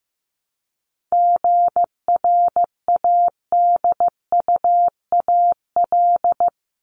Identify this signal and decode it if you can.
Morse code, heard as a keyed tone, about 15 wpm, 700 Hz. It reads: GRADUAL
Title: LY Suffix - Top 100 Words with Base Words 15wpm